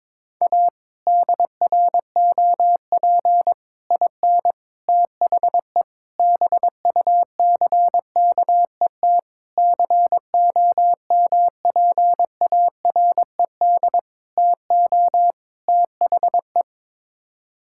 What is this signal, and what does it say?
Morse code, heard as a keyed tone, about 22 wpm, 700 Hz. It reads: A DROP IN THE BUCKET COMPARED TO THE
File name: practice / 22wpm / Sets of 8 Words - 22wpm